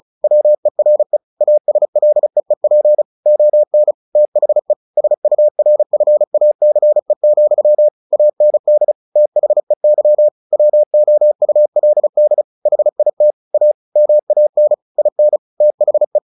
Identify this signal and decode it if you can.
Morse code, heard as a keyed tone, about 34 wpm, 600 Hz. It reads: WERE ASLEEP ON THE SURFACE, AND THEY WOULD HIT A MAN IN THE
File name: the old man and the sea 2 35wpm